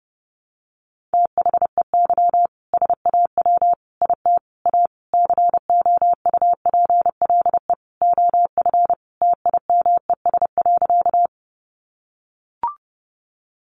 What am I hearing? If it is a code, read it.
Morse code, heard as a keyed tone, about 30 wpm, 700 Hz. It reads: THEY SAW IT A COUPLE OF TIMES.
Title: Sentences from Top 1100 Words 30wpm